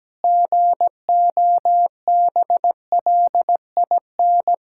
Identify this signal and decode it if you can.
Morse code, heard as a keyed tone, about 17 wpm, 700 Hz. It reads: GOBLIN
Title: Pangram Challenge v2 - 17wpm